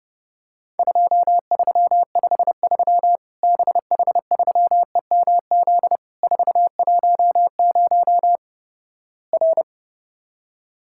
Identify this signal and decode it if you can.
Morse code, heard as a keyed tone, about 31 wpm, 700 Hz. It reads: 2353 BH3EMZ 410 R
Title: Worked All Europe DX Contest - QTC Exchange - 30wpm